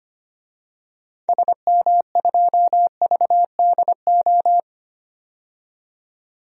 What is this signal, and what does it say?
Morse code, heard as a keyed tone, about 25 wpm, 700 Hz. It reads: SM2VDO
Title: DXCC-100 Call Signs - 6 Characters - 25wpm